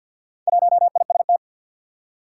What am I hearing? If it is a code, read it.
Morse code, heard as a keyed tone, about 49 wpm, 700 Hz. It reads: 1IST
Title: Four Challenging Letter-Number Combinations 50wpm